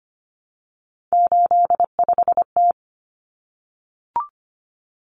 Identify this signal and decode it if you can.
Morse code, heard as a keyed tone, about 25 wpm, 700 Hz. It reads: 85T
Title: Common Cut Numbers - 3-Digit 25wpm